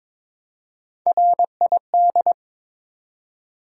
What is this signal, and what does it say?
Morse code, heard as a keyed tone, about 22 wpm, 700 Hz. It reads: RID
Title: Three Letter Words - 22wpm